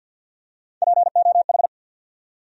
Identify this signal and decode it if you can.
Morse code, heard as a keyed tone, about 52 wpm, 700 Hz. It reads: POH
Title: USA Call Sign Postfixes - 3 Letters 50wpm